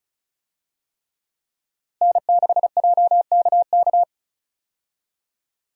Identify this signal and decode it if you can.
Morse code, heard as a keyed tone, about 36 wpm, 700 Hz. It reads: N6JKK
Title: USA Call Signs - 5 Characters - Repeated 3x - 35wpm